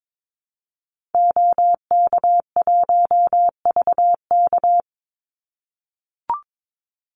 Morse code, heard as a keyed tone, about 22 wpm, 700 Hz. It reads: OK1VK
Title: Call Signs - 22wpm